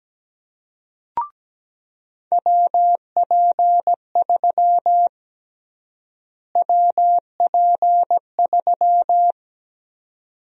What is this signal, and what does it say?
Morse code, heard as a keyed tone, about 17 wpm, 700 Hz. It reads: WP3 WP3
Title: USA Call Sign Prefixes - Repeated 2x - 17wpm